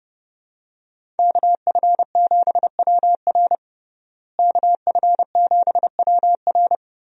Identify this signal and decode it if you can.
Morse code, heard as a keyed tone, about 30 wpm, 700 Hz. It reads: KF7WR KF7WR
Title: USA Call Signs - 5 Characters - Repeated 3x - 30wpm